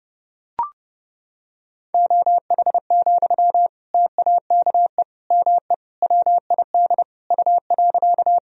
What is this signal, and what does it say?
Morse code, heard as a keyed tone, about 30 wpm, 700 Hz. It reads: OH, TAKE ME WID U.
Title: Sentences from Top 500 Words and Abbreviations 30wpm